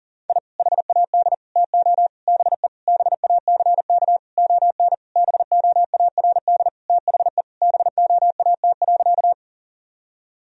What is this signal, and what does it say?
Morse code, heard as a keyed tone, about 37 wpm, 700 Hz. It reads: I HAD TO BE BACK ON BOARD THE BOAT.
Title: Sentences from Top 1000 Words - Review Part 4 - 40wpm